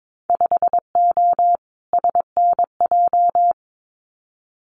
Morse code, heard as a keyed tone, about 22 wpm, 700 Hz. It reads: 5O SNJ
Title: Winter Field Day Exchanges Encore 22wpm